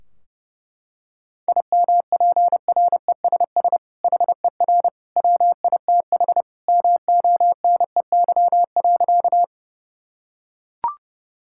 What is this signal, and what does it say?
Morse code, heard as a keyed tone, about 30 wpm, 700 Hz. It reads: IMPRESS HER WITH MONEY.
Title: Sentences from Top 1600 Words 30wpm